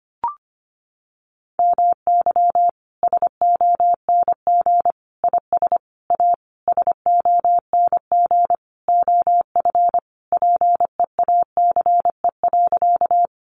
Morse code, heard as a keyed tone, about 25 wpm, 700 Hz. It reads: MY SONG IS A SONG OF PEACE.
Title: Sentences from Top 1500 Words 25wpm